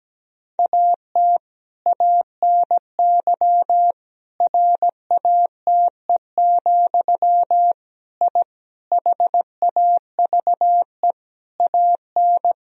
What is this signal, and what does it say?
Morse code, heard as a keyed tone, about 17 wpm, 700 Hz. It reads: AT ANY RATE, I HAVE AN
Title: Sentences from Top 1000 Words - Review Part 4 - 17wpm